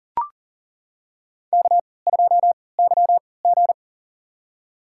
Morse code, heard as a keyed tone, about 40 wpm, spaced as for 23 wpm, 700 Hz. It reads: K2YG
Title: USA Call Signs - 3x Character Spacing - 40wpm